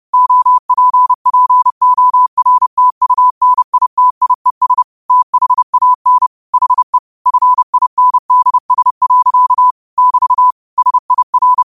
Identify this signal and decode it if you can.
Morse code, heard as a keyed tone, about 30 wpm, 1000 Hz. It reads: OPPORTUNITIES THAN HE FINDS. <BT> SIR
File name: qotdfiles / 30wpm